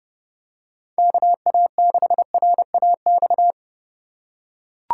Morse code, heard as a keyed tone, about 27 wpm, 700 Hz. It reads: KA6RAX